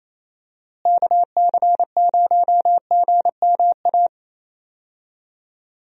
Morse code, heard as a keyed tone, about 28 wpm, 700 Hz. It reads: KC0GMA
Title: USA Call Signs - 6 Characters - Repeated 2x - 28wpm